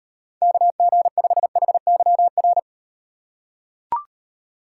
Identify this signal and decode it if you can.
Morse code, heard as a keyed tone, about 38 wpm, 700 Hz. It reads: KG5HYR